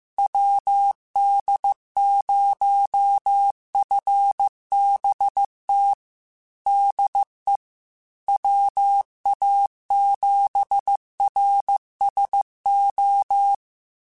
Morse code, heard as a keyed tone, about 15 wpm, 800 Hz. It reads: WD0FBT DE WA7RSO
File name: QSOs / QSOs-mp3s / WD0FBT 1@13